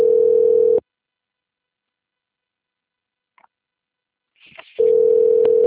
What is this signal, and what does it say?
Ringback tone (North America), 440 + 480 Hz.
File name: rg-1600-unknown- - - .25315